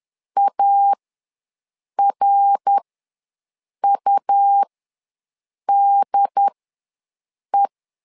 Morse code, heard as a keyed tone, about 11 wpm, spaced as for 6 wpm, 800 Hz. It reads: ARUDE
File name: LEZIONI / 3 MWRO / 3 MWR ws 10wpm 2151